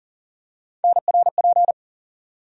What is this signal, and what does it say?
Morse code, heard as a keyed tone, about 40 wpm, 700 Hz. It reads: NRP
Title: USA Call Sign Postfixes - 3 Letters 40wpm